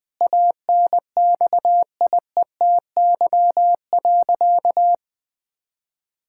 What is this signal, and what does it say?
Morse code, heard as a keyed tone, about 20 wpm, 700 Hz. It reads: ANXIETY.
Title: Sentences from Top 1600 Words 20wpm